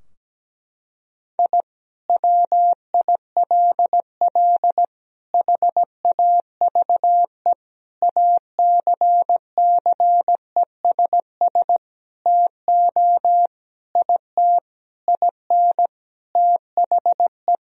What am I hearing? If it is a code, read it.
Morse code, heard as a keyed tone, about 17 wpm, 700 Hz. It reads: I WILL HAVE ACCESS TO IT IN THE
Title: Sentences from Top 1000 Words - Review Part 4 - 17wpm